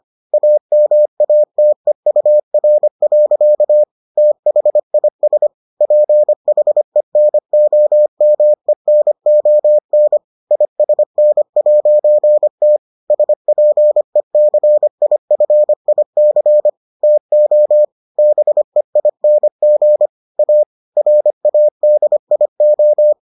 Morse code, heard as a keyed tone, about 25 wpm, 600 Hz. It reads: AMATEUR. THIS PHENOMENON ISN'T SPECIFIC TO BEING A RADIO